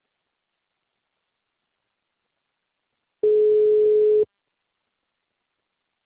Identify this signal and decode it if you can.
Ringback tone (Europe, 425 Hz).